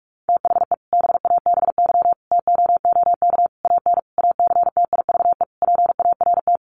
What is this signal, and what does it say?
Morse code, heard as a keyed tone, about 44 wpm, 700 Hz. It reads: THE BABY TOOK AN ACTIVE PART
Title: Sentences from Top 1300 Words 45wpm